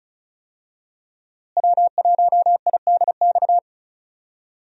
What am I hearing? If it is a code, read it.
Morse code, heard as a keyed tone, about 36 wpm, 700 Hz. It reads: W1IDX
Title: USA Call Signs - 5 Characters - Repeated 2x - 35wpm